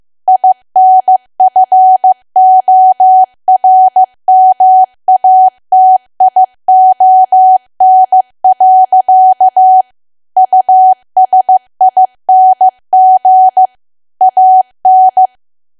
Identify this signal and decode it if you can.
Morse code, heard as a keyed tone, about 15 wpm, 750 Hz. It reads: INFORMATION. USING AN
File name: cwa / W1AW-Level2 / 15WPM